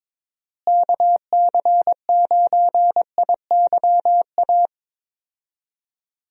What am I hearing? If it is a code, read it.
Morse code, heard as a keyed tone, about 22 wpm, 700 Hz. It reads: KC9IYA
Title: USA Call Signs - 6 Characters - Repeated 3x - 22wpm